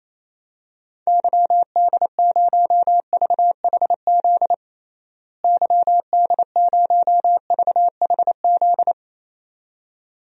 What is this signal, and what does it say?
Morse code, heard as a keyed tone, about 27 wpm, 700 Hz. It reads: YD0VHZ YD0VHZ
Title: DXCC-100 Call Signs - 6 Characters - Repeated 2x - 28wpm